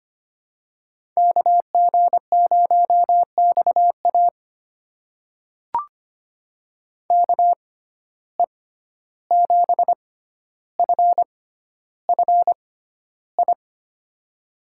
Morse code, heard as a keyed tone, about 25 wpm, spaced as for 12 wpm, 700 Hz. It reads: KG0XA KE7FFI
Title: USA Call Signs - 6x Character Spacing - 25wpm